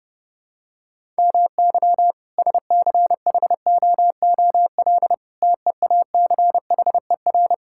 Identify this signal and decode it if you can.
Morse code, heard as a keyed tone, about 30 wpm, 700 Hz. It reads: MY SCHOOL TEACHER
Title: Sentences from Top 1000 Words - Review Part 6 - 30wpm